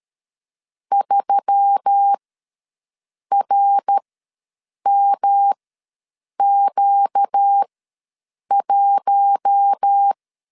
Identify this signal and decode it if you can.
Morse code, heard as a keyed tone, about 13 wpm, spaced as for 7 wpm, 800 Hz. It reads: 3RMQ1
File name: LEZIONI / 5 LBVC / 5 LBVC ws 12wpm 3159